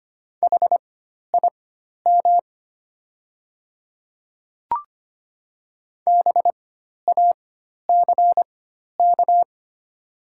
Morse code, heard as a keyed tone, about 25 wpm, spaced as for 12 wpm, 700 Hz. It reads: HIM BACK
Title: 4x Character Spacing - Top 300 Words - 25wpm